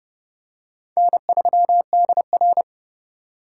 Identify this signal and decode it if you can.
Morse code, heard as a keyed tone, about 30 wpm, 700 Hz. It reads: N3DR